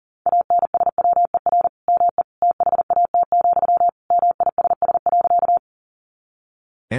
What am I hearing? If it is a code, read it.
Morse code, heard as a keyed tone, about 40 wpm, 700 Hz. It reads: ANSWER ME THAT, MISS.
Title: Sentences from Top 500 Words 40wpm